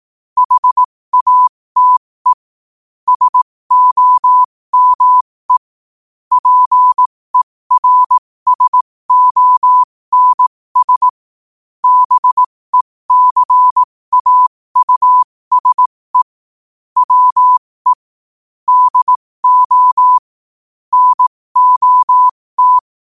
Morse code, heard as a keyed tone, about 18 wpm, spaced as for 15 wpm, 1000 Hz. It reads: HATE SOME PERSONS BECAUSE WE DO NOT
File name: qotdfiles / 15wpm